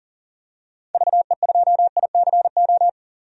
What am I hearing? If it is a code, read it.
Morse code, heard as a keyed tone, about 40 wpm, 700 Hz. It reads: VE2ICO
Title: DXCC-100 Call Signs - 6 Characters - 40wpm